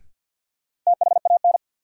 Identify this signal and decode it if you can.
Morse code, heard as a keyed tone, about 48 wpm, 700 Hz. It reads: THAN
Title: Sentences from Top 100 Words - New Word Drill - 50wpm